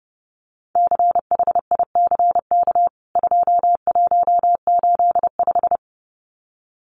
Morse code, heard as a keyed tone, about 30 wpm, 700 Hz. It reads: CHICK 2185
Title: CWT Exchange 30wpm